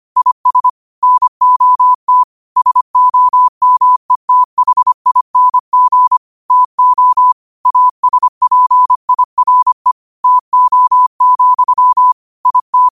Morse code, heard as a keyed tone, about 25 wpm, 1000 Hz. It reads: IS NOT SOMETHING TO ASPIRE TO, IT
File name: qotdfiles / 25wpm